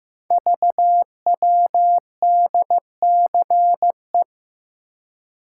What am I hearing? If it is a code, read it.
Morse code, heard as a keyed tone, about 15 wpm, 700 Hz. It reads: VWDCE
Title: Code Groups - 5 Letters - 15wpm